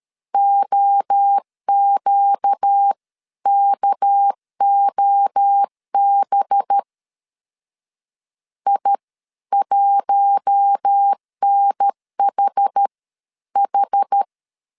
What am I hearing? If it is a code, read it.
Morse code, heard as a keyed tone, about 13 wpm, spaced as for 11 wpm, 800 Hz. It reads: OQKOB I1NHH
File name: CW / DET / 1 6 1500C S A 12wpm 3440